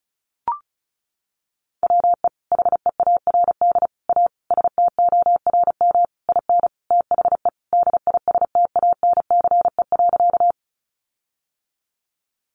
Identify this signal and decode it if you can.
Morse code, heard as a keyed tone, about 34 wpm, 700 Hz. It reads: WE HEARD A STORM IN THE DISTANCE.
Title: Sentences from Top 1500 Words 35wpm